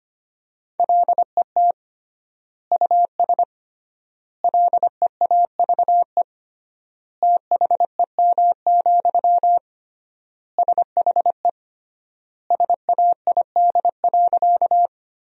Morse code, heard as a keyed tone, about 25 wpm, 700 Hz. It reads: LET US LEAVE THEM, SHE SAID.
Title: 3x Word-Spacing - Sentences from Top 400 Words - 25wpm